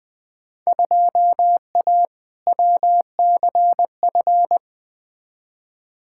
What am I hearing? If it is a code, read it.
Morse code, heard as a keyed tone, about 20 wpm, 700 Hz. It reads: 2A WCF
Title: Field Day Exchanges - 20wpm